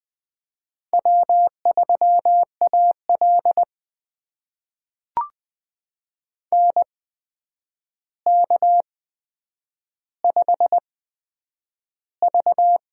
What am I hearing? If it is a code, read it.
Morse code, heard as a keyed tone, about 20 wpm, spaced as for 9 wpm, 700 Hz. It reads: W3AL NK5V
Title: USA Call Signs - 4 Characters - 8x Character Spacing - 20wpm